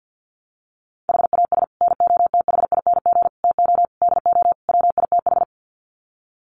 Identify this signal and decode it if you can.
Morse code, heard as a keyed tone, about 48 wpm, 700 Hz. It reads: HAS NOTHING TO DO WITH
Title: Sets of 5 Words 50wpm